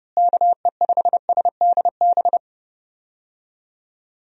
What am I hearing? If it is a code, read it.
Morse code, heard as a keyed tone, about 30 wpm, 700 Hz. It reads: KE5SDB
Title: USA Call Signs - 6 Characters - Repeated 2x - 30wpm